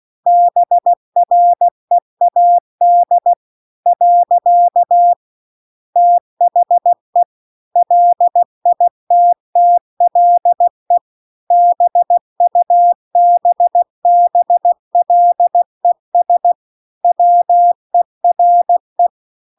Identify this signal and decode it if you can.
Morse code, heard as a keyed tone, about 16 wpm, 700 Hz. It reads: BREAD . THE LITTLE BUBBLES WERE